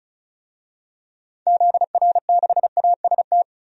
Morse code, heard as a keyed tone, about 36 wpm, 700 Hz. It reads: ZR6AST
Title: DXCC-100 Call Signs - 6 Characters - 35wpm